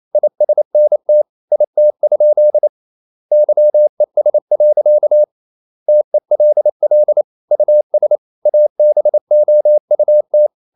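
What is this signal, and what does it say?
Morse code, heard as a keyed tone, about 28 wpm, 600 Hz. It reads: ISNT IT? YES. TELL US ABOUT
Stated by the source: Morse code practice at 28 wpm, text of Ken Follett's novel Never in English, from CW med Gnister.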